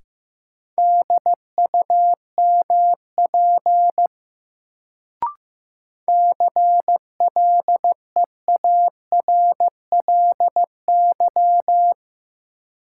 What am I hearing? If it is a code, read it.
Morse code, heard as a keyed tone, about 16 wpm, 700 Hz. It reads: DUMP CLEARLY
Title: Sentences from Top 1300 Words - New Word Drill Alt - 15wpm